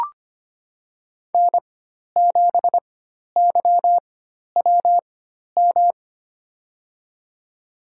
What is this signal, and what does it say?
Morse code, heard as a keyed tone, about 25 wpm, spaced as for 12 wpm, 700 Hz. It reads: N7YWM
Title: USA Call Signs - 4x Character Spacing - 25wpm